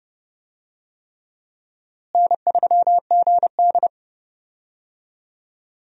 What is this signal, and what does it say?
Morse code, heard as a keyed tone, about 30 wpm, 700 Hz. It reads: N3GD
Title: USA Call Signs - 4 Characters - Repeated 3x - 30wpm